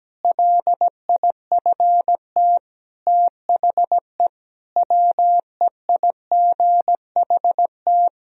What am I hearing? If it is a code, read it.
Morse code, heard as a keyed tone, about 17 wpm, 700 Hz. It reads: LIFT THE WEIGHT
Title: Sentences from Top 1000 Words - Review Part 3 - 17wpm